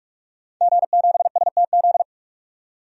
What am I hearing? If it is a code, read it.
Morse code, heard as a keyed tone, about 46 wpm, 700 Hz. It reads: G7STZ
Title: DXCC-100 Call Signs - 5 Characters - 45wpm